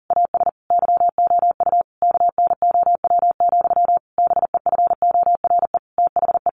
Morse code, heard as a keyed tone, about 40 wpm, 700 Hz. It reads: AS YOU KNOW, BEFORE THE
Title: A Sound Of Thunder 40wpm